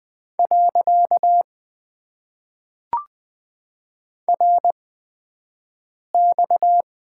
Morse code, heard as a keyed tone, about 20 wpm, spaced as for 5 wpm, 700 Hz. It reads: . RX
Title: Top 90 QSO Elements - 8x Character Spacing - 20wpm